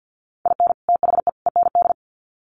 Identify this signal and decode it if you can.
Morse code, heard as a keyed tone, about 50 wpm, 700 Hz. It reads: IN THE END
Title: Sets of 3 Words 50wpm